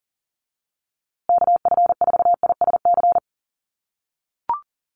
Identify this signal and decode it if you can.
Morse code, heard as a keyed tone, about 40 wpm, 700 Hz. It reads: KF4ISC
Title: USA Call Signs Repeated 3x 40wpm